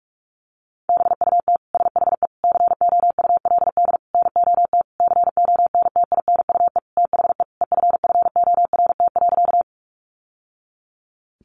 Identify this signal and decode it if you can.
Morse code, heard as a keyed tone, about 44 wpm, 700 Hz. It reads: BUT SHE COULD NOT CONTINUE THE EFFORT.
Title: Sentences from Top 1200 Words 45wpm